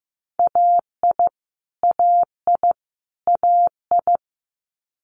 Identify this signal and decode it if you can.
Morse code, heard as a keyed tone, about 15 wpm, 700 Hz. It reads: AI AI AI
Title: Multi-letter Phonemes Repeated 3x 15wpm